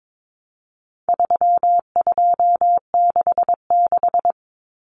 Morse code, heard as a keyed tone, about 22 wpm, 700 Hz. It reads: 3266
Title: Numbers - 4-Digit 22wpm